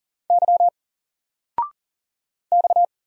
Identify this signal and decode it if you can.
Morse code, heard as a keyed tone, about 40 wpm, 700 Hz. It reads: Y X
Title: Single Letter-Number Alt 40wpm